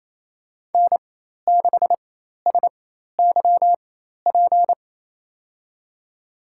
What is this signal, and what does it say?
Morse code, heard as a keyed tone, about 28 wpm, spaced as for 13 wpm, 700 Hz. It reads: N6SYP
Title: USA Call Signs - 5 Characters - 4x Character Spacing - 28wpm